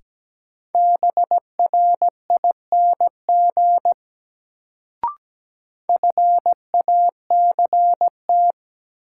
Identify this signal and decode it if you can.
Morse code, heard as a keyed tone, about 17 wpm, 700 Hz. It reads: BRING FACT
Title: Sentences from Top 500 Words - New Word Drill - 17wpm